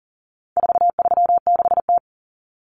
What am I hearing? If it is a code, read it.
Morse code, heard as a keyed tone, about 40 wpm, 700 Hz. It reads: 436T
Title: Common Cut Numbers - 4-Digit 40wpm